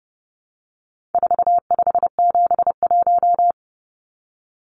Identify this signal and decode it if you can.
Morse code, heard as a keyed tone, about 30 wpm, 700 Hz. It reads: 4571
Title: Numbers - 4-Digit 30wpm